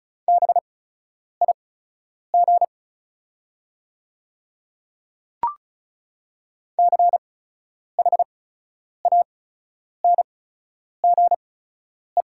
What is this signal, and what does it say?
Morse code, heard as a keyed tone, about 34 wpm, spaced as for 10 wpm, 690 Hz. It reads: BIG CHANGE
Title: 8x Character Spacing - Top 200 Words - 35wpm